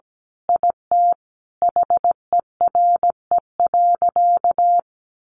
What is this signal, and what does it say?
Morse code, heard as a keyed tone, about 17 wpm, 700 Hz. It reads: IT HERE.
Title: Sentences from Top 1100 Words 17wpm